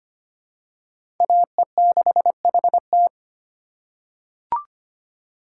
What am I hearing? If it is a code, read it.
Morse code, heard as a keyed tone, about 25 wpm, 700 Hz. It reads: AE6HT
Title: USA Call Signs - 5 Characters - 3x Character Spacing - 25wpm